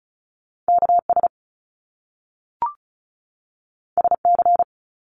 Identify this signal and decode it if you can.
Morse code, heard as a keyed tone, about 36 wpm, 700 Hz. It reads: KS SC
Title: US States 35wpm